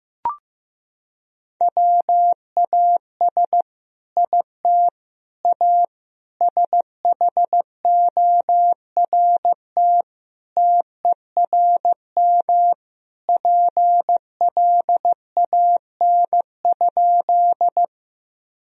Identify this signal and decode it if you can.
Morse code, heard as a keyed tone, about 15 wpm, 700 Hz. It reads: WAS IT A SHORT TERM PLAN?
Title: Sentences from Top 1000 Words - Review Part 4 - 15wpm